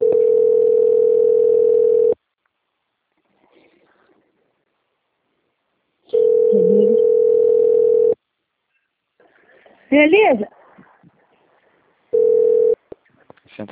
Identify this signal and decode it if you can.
Ringback tone (North America), 440 + 480 Hz, with speech.
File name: out- - - - - .29556